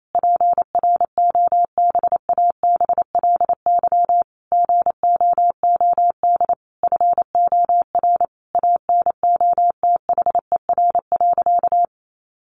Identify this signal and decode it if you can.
Morse code, heard as a keyed tone, about 28 wpm, 700 Hz. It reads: PROBABLY GOOD FOR ANOTHER.
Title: Sentences from Top 1400 Words 28wpm